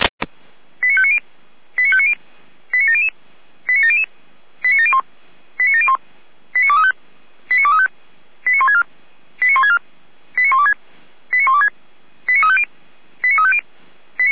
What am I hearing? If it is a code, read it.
Five-tone selective call (ZVEI), heard as quick runs of five tones, tones of about 70 ms. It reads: 89590 89590 89700 89700 89701 89701 89235 89235 89164 89164 89126 89126 89350 89350